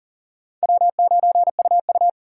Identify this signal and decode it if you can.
Morse code, heard as a keyed tone, about 40 wpm, 700 Hz. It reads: W9UU
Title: USA Call Signs - 4 Characters - 3x Character Spacing - 40wpm